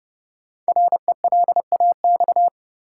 Morse code, heard as a keyed tone, about 30 wpm, 700 Hz. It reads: RELAX
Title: Sentences from Top 1400 Words - New Word Drill - 30wpm